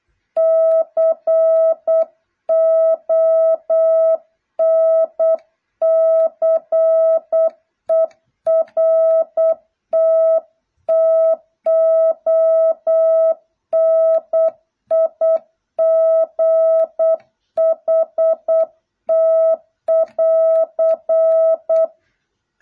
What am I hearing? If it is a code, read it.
Morse code, heard as a keyed tone, about 8 wpm, 650 Hz. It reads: CONCERT TONIGHT <AR>